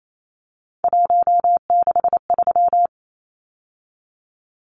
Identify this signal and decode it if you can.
Morse code, heard as a keyed tone, about 28 wpm, 700 Hz. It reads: 163
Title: Numbers - 3-Digit 28wpm